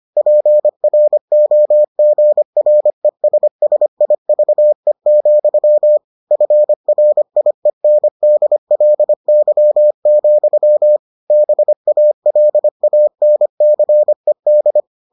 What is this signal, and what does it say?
Morse code, heard as a keyed tone, about 25 wpm, 600 Hz. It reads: PROGRESSIVE, FRIENDLY, BALANCED